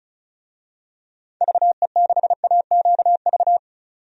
Morse code, heard as a keyed tone, about 34 wpm, 700 Hz. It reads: VE6AQV
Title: DXCC-100 Call Signs - 6 Characters - 35wpm